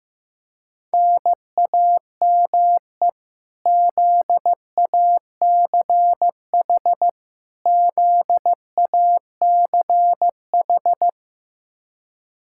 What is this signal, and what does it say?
Morse code, heard as a keyed tone, about 15 wpm, 700 Hz. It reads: NAME ZACH ZACH
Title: Practice QSO - Exchange 1 - Beginning - 15wpm